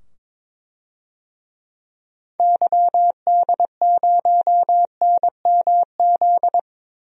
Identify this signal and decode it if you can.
Morse code, heard as a keyed tone, about 22 wpm, 700 Hz. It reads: YD0NMZ
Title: DXCC-100 Call Signs - 6 Characters - 22wpm